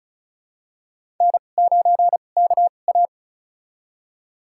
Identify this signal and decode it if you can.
Morse code, heard as a keyed tone, about 36 wpm, spaced as for 25 wpm, 700 Hz. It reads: N9KA
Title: USA Call Signs - 4 Characters - 2x Character Spacing - 35wpm